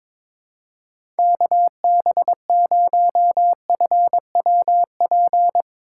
Morse code, heard as a keyed tone, about 22 wpm, 700 Hz. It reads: KB0FWP
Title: USA Call Signs - 6 Characters - 4x Character Spacing - 22wpm